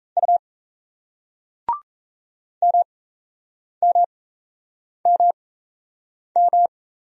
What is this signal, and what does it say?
Morse code, heard as a keyed tone, about 36 wpm, spaced as for 8 wpm, 700 Hz. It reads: U MMMM